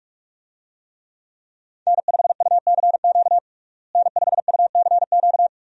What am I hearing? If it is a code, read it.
Morse code, heard as a keyed tone, about 41 wpm, 700 Hz. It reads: N5UCQ N5UCQ
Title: USA Call Signs - 5 Characters - Repeated 3x - 45wpm